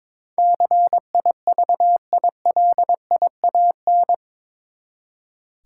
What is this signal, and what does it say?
Morse code, heard as a keyed tone, about 22 wpm, 700 Hz. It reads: CIVILIAN
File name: practice / 22wpm / Eight Letter Words - 22wpm